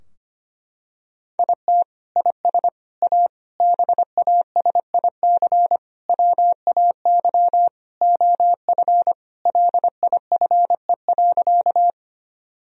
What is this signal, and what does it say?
Morse code, heard as a keyed tone, about 25 wpm, 700 Hz. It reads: IT IS A BASIC WAY OF LIFE.
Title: Sentences from Top 1000 Words - Review Part 3 - 25wpm